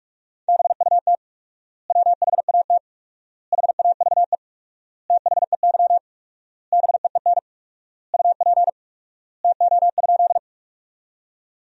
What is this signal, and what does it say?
Morse code, heard as a keyed tone, about 46 wpm, 700 Hz. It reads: BUT WHAT HAVE THEY BEEN UP TO?
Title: 4x Word-Spacing - Sentences from Top 100 Words - 45wpm